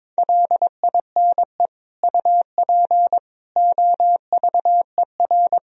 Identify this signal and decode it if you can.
Morse code, heard as a keyed tone, about 22 wpm, 700 Hz. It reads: LINE UP OVER
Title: Sentences from Top 1000 Words - Review Part 1 - 22wpm